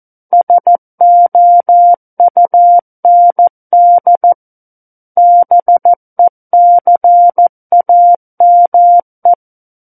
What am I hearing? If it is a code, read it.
Morse code, heard as a keyed tone, about 14 wpm, 700 Hz. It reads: SOUND BECAME
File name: Book One / 05-Chapter 5 - 14 WPM